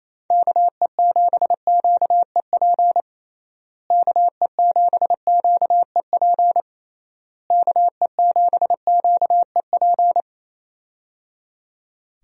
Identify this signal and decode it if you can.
Morse code, heard as a keyed tone, about 27 wpm, 700 Hz. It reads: KE7QEP KE7QEP KE7QEP
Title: USA Call Signs - 6 Characters - Repeated 3x - 28wpm